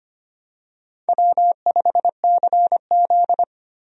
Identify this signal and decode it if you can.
Morse code, heard as a keyed tone, about 25 wpm, 700 Hz. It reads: W5CZ
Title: USA Call Signs - 4 Characters - Repeated 2x - 25wpm